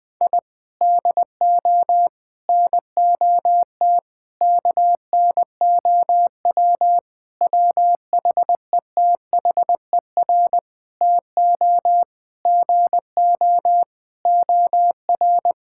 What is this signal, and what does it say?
Morse code, heard as a keyed tone, about 21 wpm, 700 Hz. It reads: I DO NOT KNOW WHETHER TO GO OR
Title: Sentences with 5-Word Phrases - 20wpm